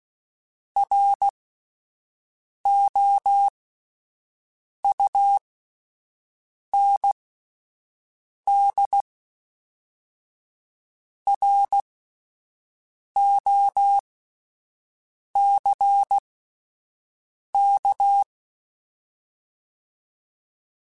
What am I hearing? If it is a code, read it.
Morse code, heard as a keyed tone, about 16 wpm, spaced as for 5 wpm, 800 Hz. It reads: ROUND ROCK